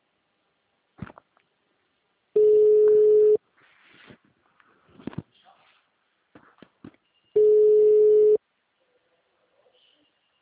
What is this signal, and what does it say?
Ringback tone (Europe, 425 Hz).
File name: out- -2003- - - .56608